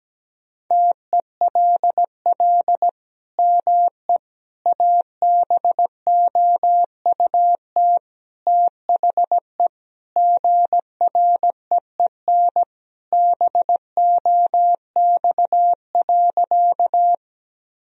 Morse code, heard as a keyed tone, about 17 wpm, 700 Hz. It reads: TELL ME ABOUT THE GREEN BOX.
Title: Sentences from Top 1000 Words - Review Part 2 - 17wpm